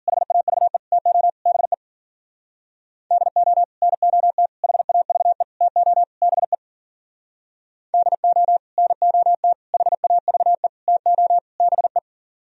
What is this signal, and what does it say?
Morse code, heard as a keyed tone, about 50 wpm, 700 Hz. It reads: HAVE TO BE DO NOT HAVE TO BE DO NOT HAVE TO BE